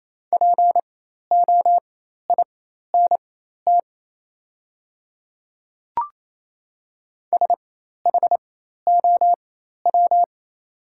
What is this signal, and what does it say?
Morse code, heard as a keyed tone, about 28 wpm, spaced as for 13 wpm, 700 Hz. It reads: POINT SHOW
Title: 4x Character Spacing - Top 300 Words - 28wpm